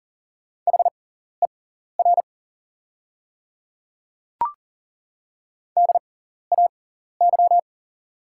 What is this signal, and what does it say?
Morse code, heard as a keyed tone, about 40 wpm, spaced as for 14 wpm, 700 Hz. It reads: HER DAY
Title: 6x Character Spacing - Top 200 Words - 40wpm